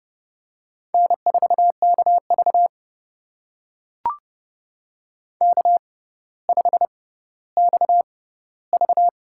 Morse code, heard as a keyed tone, about 30 wpm, spaced as for 15 wpm, 700 Hz. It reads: N4KV K5XV
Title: USA Call Signs - 4 Characters - 6x Character Spacing - 30wpm